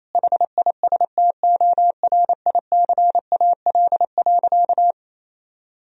Morse code, heard as a keyed tone, about 28 wpm, 700 Hz. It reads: HISTORICAL.
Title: Sentences from Top 1000 Words - Review Part 4 - 28wpm